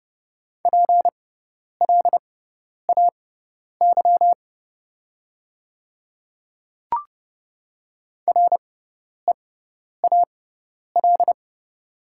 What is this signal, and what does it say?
Morse code, heard as a keyed tone, about 31 wpm, spaced as for 10 wpm, 700 Hz. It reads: PLAY REAL
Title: 6x Character Spacing - Top 300 Words - 30wpm